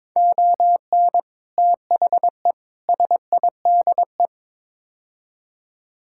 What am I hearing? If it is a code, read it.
Morse code, heard as a keyed tone, about 22 wpm, 700 Hz. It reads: ON THE SIDE
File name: practice / 22wpm / Build A Phrase - 5-Word Phrases - 22wpm